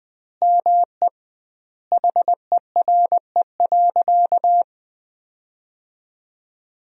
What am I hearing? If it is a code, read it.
Morse code, heard as a keyed tone, about 21 wpm, 700 Hz. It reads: ME HERE.
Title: 2x Word-Spacing - Sentences from Top 200 Words - 20wpm